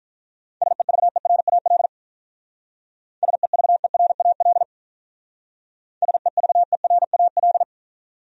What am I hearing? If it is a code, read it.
Morse code, heard as a keyed tone, about 43 wpm, 700 Hz. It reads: SEVERAL SEVERAL SEVERAL
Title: Speed-Racing Top 500 Words 35wpm